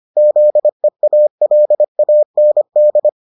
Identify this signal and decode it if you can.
Morse code, heard as a keyed tone, about 25 wpm, 600 Hz. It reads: ZEALAND